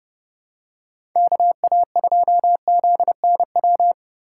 Morse code, heard as a keyed tone, about 30 wpm, 700 Hz. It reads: KA2ZNW
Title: USA Call Signs - 6 Characters - 2x Character Spacing - 30wpm